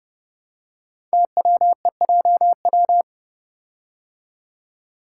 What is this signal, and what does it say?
Morse code, heard as a keyed tone, about 30 wpm, 690 Hz. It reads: TWEJW
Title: Code Groups - 5 Letters - 30wpm